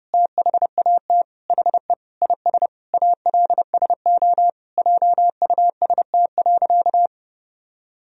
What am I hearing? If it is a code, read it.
Morse code, heard as a keyed tone, about 32 wpm, 700 Hz. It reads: THAT HE IS ALSO JUST.
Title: Sentences from Top 1000 Words - Review Part 1 - 30wpm